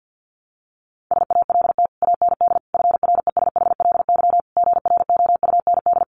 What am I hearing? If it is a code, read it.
Morse code, heard as a keyed tone, about 50 wpm, 700 Hz. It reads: SALT AND FRESHLY GROUND
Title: Sets of 5 Words Encore 50wpm